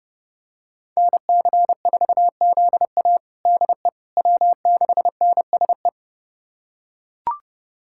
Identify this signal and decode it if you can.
Morse code, heard as a keyed tone, about 29 wpm, 700 Hz. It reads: NC4ZA DE W6NSE
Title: USA Callsign DE Callsign - 5 Characters - 30wpm